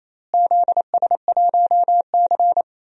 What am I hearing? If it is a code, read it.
Morse code, heard as a keyed tone, about 27 wpm, 700 Hz. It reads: ZS1C
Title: DXCC-100 Call Signs - 4 Characters - Repeated 3x - 28wpm